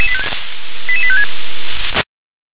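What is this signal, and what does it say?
Five-tone selective call (ZVEI), heard as quick runs of five tones, tones of about 70 ms. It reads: 88856
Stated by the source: ZVEI five-tone call received by SDR.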